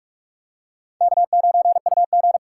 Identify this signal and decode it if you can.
Morse code, heard as a keyed tone, about 44 wpm, 700 Hz. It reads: K9UG
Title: USA Call Signs - 2x Character Spacing - 45wpm